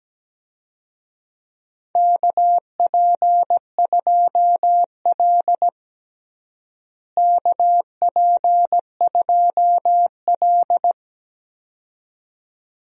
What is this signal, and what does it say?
Morse code, heard as a keyed tone, about 17 wpm, 700 Hz. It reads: KP2L KP2L
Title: DXCC-100 Call Signs - 4 Characters - Repeated 2x - 17wpm